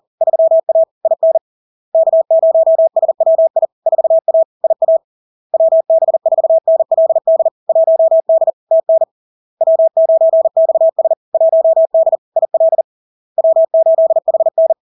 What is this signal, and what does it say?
Morse code, heard as a keyed tone, about 40 wpm, 650 Hz. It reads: 3A IN K0SWI 4A IA WB4NLD 1D TN W9XS 1D IL W8HN
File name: LARK-FD-2022-Log-in-CW-40wpm-650-Hz